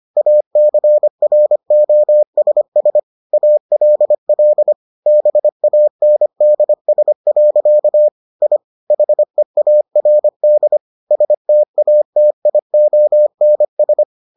Morse code, heard as a keyed tone, about 25 wpm, 600 Hz. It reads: ACROSS ALL BANDS. I HEARD STATIONS